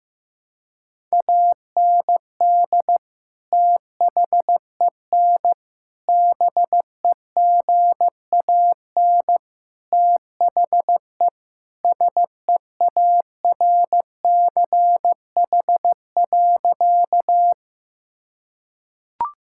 Morse code, heard as a keyed tone, about 15 wpm, 700 Hz. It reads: AND THEN BEGAN THE SEARCH.
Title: Sentences from Top 1000 Words - Review Part 5 - 15wpm